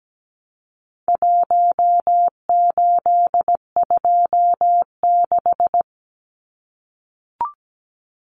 Morse code, heard as a keyed tone, about 17 wpm, 700 Hz. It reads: 1826
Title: Numbers - Contest Style - 4-Digit 17wpm